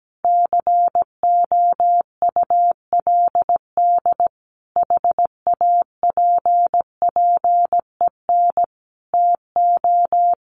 Morse code, heard as a keyed tone, about 17 wpm, 700 Hz. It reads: COULD HAPPEN TO
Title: Sentences from Top 400 Words 17wpm